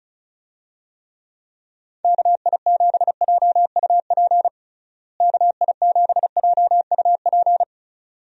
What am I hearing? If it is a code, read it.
Morse code, heard as a keyed tone, about 36 wpm, 700 Hz. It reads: KI7JUP KI7JUP
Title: USA Call Signs - 6 Characters - Repeated 2x - 35wpm